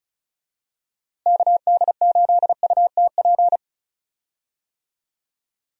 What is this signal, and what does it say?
Morse code, heard as a keyed tone, about 34 wpm, 700 Hz. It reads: KD8UTP
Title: USA Call Signs - 6 Characters - Repeated 2x - 35wpm